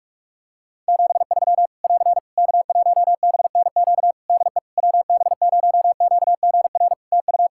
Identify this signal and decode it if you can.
Morse code, heard as a keyed tone, about 46 wpm, 700 Hz. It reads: 73 <AR> K1BNQ DE WB0QGR TU
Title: Practice QSO - Exchange 4 - Ending - 45wpm